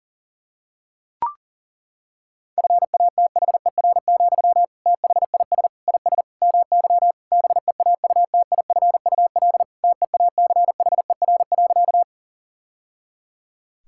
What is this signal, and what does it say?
Morse code, heard as a keyed tone, about 40 wpm, 700 Hz. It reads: FATHER, THIS IS MY BEAUTIFUL TEACHER.
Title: Sentences from Top 1000 Words - Review Part 6 - 40wpm